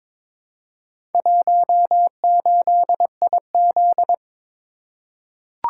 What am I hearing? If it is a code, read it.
Morse code, heard as a keyed tone, about 23 wpm, 700 Hz. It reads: 18IZ
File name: practice / 22wpm / Four Letter-Number Combinations - 22wpm